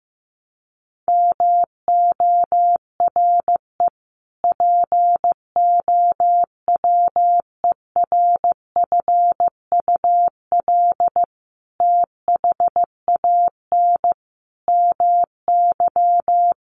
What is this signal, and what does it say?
Morse code, heard as a keyed tone, about 15 wpm, 700 Hz. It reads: MORE POWERFUL THAN MY
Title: Sentences from Top 1300 Words 15wpm